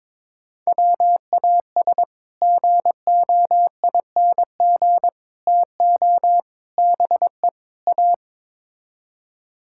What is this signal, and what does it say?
Morse code, heard as a keyed tone, about 21 wpm, 700 Hz. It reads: WAS GOING TO BE A
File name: practice / 22wpm / Build A Phrase - 5-Word Phrases - 22wpm